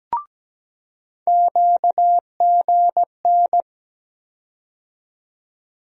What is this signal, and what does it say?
Morse code, heard as a keyed tone, about 17 wpm, 700 Hz. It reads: QGN
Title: USA Call Sign Postfixes - 3 Letters 17wpm